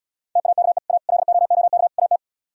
Morse code, heard as a keyed tone, about 77 wpm, 700 Hz. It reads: TAKE A LOOK AT